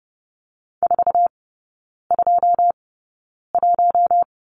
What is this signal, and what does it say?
Morse code, heard as a keyed tone, about 30 wpm, spaced as for 9 wpm, 700 Hz. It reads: 421
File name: Lesson 16e - Letter c - Three Characters 30wpm